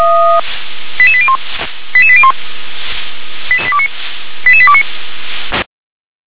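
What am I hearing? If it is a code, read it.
Five-tone selective call (ZVEI), heard as quick runs of five tones, tones of about 70 ms. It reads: 88801 88801 88829
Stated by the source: ZVEI five-tone selective call 88100, received by SDR.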